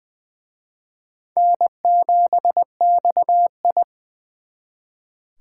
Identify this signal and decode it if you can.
Morse code, heard as a keyed tone, about 24 wpm, 700 Hz. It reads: N7XI
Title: Speed Racing - USA 1x2 Call Signs 20wpm